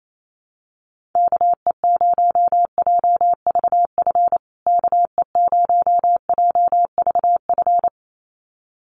Morse code, heard as a keyed tone, about 28 wpm, 700 Hz. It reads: KE0JVF KE0JVF
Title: USA Call Signs Repeated 2x 28wpm